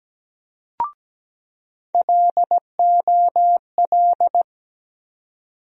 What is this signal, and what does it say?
Morse code, heard as a keyed tone, about 17 wpm, 700 Hz. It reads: LOL
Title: USA Call Sign Postfixes - 3 Letters 17wpm